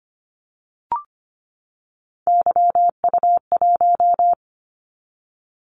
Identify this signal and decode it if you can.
Morse code, heard as a keyed tone, about 25 wpm, 700 Hz. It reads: YU1
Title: Callsign Prefixes for Top 25 DXCC - 25wpm